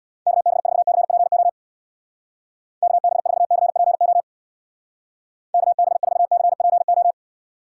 Morse code, heard as a keyed tone, about 61 wpm, 700 Hz. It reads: KB4CPO KB4CPO KB4CPO
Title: Speed Racing - USA 2x3 Call Signs 50wpm